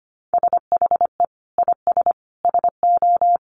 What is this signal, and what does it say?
Morse code, heard as a keyed tone, about 25 wpm, 700 Hz. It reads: SHE IS SO
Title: Sentences from Top 1000 Words 25wpm